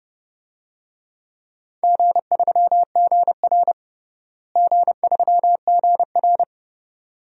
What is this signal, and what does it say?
Morse code, heard as a keyed tone, about 30 wpm, 700 Hz. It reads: G3GR G3GR
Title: DXCC-100 Call Signs - 4 Characters - Repeated 2x - 30wpm